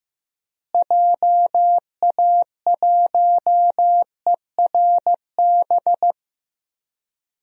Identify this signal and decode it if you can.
Morse code, heard as a keyed tone, about 15 wpm, 700 Hz. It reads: JA1ERB
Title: DXCC-100 Call Signs - 6 Characters - Repeated 3x - 15wpm